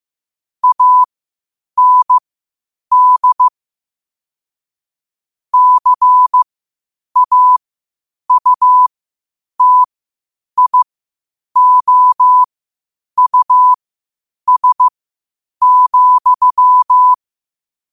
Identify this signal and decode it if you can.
Morse code, heard as a keyed tone, about 15 wpm, spaced as for 8 wpm, 1000 Hz. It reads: AND CAUTIOUS,
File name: qotdfiles / 8wpm